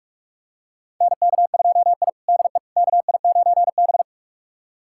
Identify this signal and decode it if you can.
Morse code, heard as a keyed tone, about 44 wpm, 690 Hz. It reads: NK2I DE KI9B
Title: USA Callsign DE Callsign - 4 Characters - 45wpm